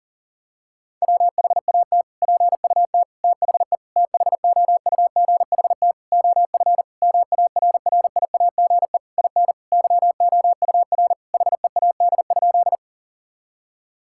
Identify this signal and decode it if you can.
Morse code, heard as a keyed tone, about 40 wpm, 700 Hz. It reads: WHAT PUT THE THOUGHT OF MARRIAGE IN YOUR HEAD?
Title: Sentences from Top 1000 Words - Review Part 5 - 40wpm